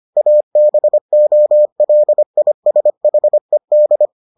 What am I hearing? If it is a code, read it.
Morse code, heard as a keyed tone, about 25 wpm, 600 Hz. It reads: ABOLISHED